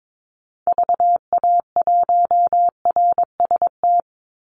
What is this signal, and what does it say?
Morse code, heard as a keyed tone, about 22 wpm, 690 Hz. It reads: VA1RST
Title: Call Signs Repeated 3x - 22wpm